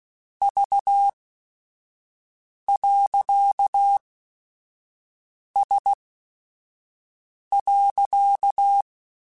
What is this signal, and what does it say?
Morse code, heard as a keyed tone, about 16 wpm, spaced as for 5 wpm, 800 Hz. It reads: V.S.